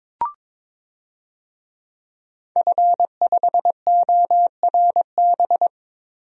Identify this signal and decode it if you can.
Morse code, heard as a keyed tone, about 22 wpm, 700 Hz. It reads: F5ORB
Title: DXCC-100 Call Signs - 5 Characters - Repeated 2x - 22wpm